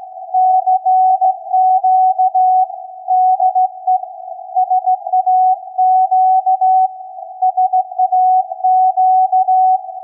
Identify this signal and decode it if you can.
Morse code, heard as a keyed tone, about 16 wpm, 740 Hz. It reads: CQ DE SAQ SAQ
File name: iz7slz / VLF / SAQ 16-11-2022